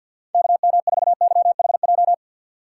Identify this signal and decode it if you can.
Morse code, heard as a keyed tone, about 52 wpm, 700 Hz. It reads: KM4YHJ
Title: USA Call Signs - 3x Character Spacing - 50wpm